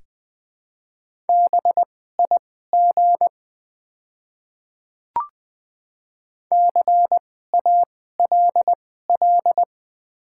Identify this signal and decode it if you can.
Morse code, heard as a keyed tone, about 20 wpm, spaced as for 14 wpm, 700 Hz. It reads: BIG CALL
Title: 2x Character Spacing - Top 200 Words - 20wpm